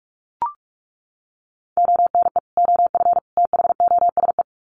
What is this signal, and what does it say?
Morse code, heard as a keyed tone, about 44 wpm, 700 Hz. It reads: ONE OF THOSE
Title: Sets of 3 Words 45wpm